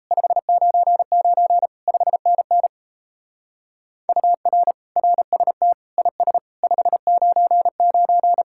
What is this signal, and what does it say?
Morse code, heard as a keyed tone, about 32 wpm, 700 Hz. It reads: 599 5NN UR RST IS 599
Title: Speed-Racing Common QSO Phrases - 28wpm